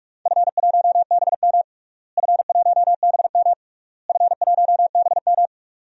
Morse code, heard as a keyed tone, about 46 wpm, 700 Hz. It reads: F1BM F1BM F1BM
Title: DXCC-100 Call Signs - 4 Characters - Repeated 3x - 45wpm